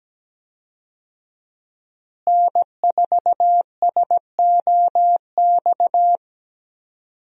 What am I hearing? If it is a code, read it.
Morse code, heard as a keyed tone, about 17 wpm, 700 Hz. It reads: N4SOX
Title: USA Call Signs - 5 Characters - Repeated 3x - 17wpm